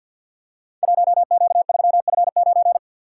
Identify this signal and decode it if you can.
Morse code, heard as a keyed tone, about 46 wpm, 700 Hz. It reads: 1Q3F9
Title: Code Groups - 5 Letter:Numbers - 50wpm